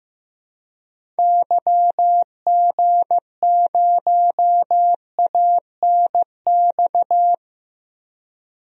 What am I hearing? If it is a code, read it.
Morse code, heard as a keyed tone, about 15 wpm, 700 Hz. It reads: YG0ANX
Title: DXCC-100 Call Signs - 6 Characters - 15wpm